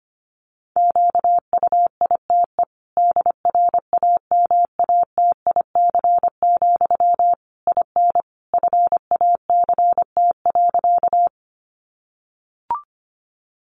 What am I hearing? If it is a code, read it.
Morse code, heard as a keyed tone, about 25 wpm, 700 Hz. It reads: QUITE DRAMATIC, IN FACT.
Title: Sentences from Top 1300 Words 25wpm